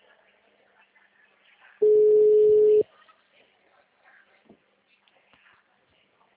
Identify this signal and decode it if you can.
Ringback tone (Europe, 425 Hz).
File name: out- -2004- - - .30046